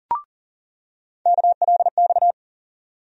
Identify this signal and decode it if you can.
Morse code, heard as a keyed tone, about 40 wpm, 700 Hz. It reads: KLX
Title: USA Call Sign Postfixes - 3 Letters 40wpm